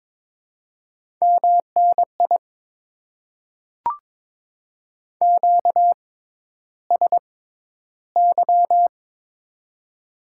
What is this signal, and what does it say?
Morse code, heard as a keyed tone, about 22 wpm, spaced as for 11 wpm, 700 Hz. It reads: MNI QSY
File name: Top 90 QSO Elements - 6x Character Spacing - 22wpm